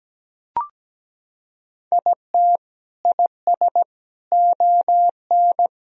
Morse code, heard as a keyed tone, about 17 wpm, 700 Hz. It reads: IT IS ON
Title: Sentences from Top 1000 Words - Review Part 2 - 17wpm